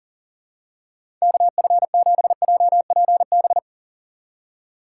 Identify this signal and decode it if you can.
Morse code, heard as a keyed tone, about 40 wpm, 700 Hz. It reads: KF7JPB
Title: USA Call Signs - 6 Characters - 3x Character Spacing - 40wpm